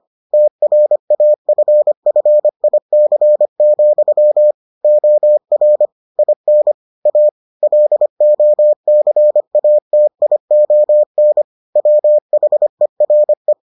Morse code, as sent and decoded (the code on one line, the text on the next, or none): - .-. .- ..-. ..-. .. -.-. --..-- / --- .-. / .. -. / .- / .-.. --- -.-. .- - .. --- -. / .-- .... .-. .
TRAFFIC, OR IN A LOCATION WHERE